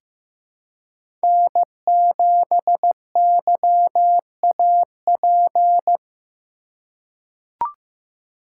Morse code, heard as a keyed tone, about 15 wpm, 700 Hz. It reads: N7YAP
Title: USA Call Signs - 5 Characters - 4x Character Spacing - 15wpm